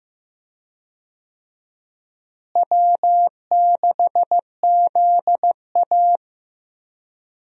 Morse code, heard as a keyed tone, about 15 wpm, 700 Hz. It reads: W6ZA
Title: USA Call Signs - 4 Characters - Repeated 2x - 15wpm